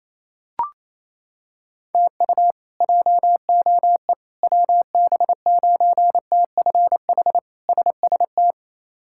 Morse code, heard as a keyed tone, about 28 wpm, 700 Hz. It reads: TU JOE WB9TFH SST
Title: K1USN SST QSO Practice - 28wpm